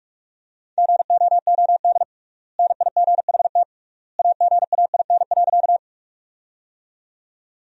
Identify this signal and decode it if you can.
Morse code, heard as a keyed tone, about 46 wpm, 700 Hz. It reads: GOOD NIGHT AGAIN.
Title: 3x Word-Spacing - Sentences from Top 300 Words - 45wpm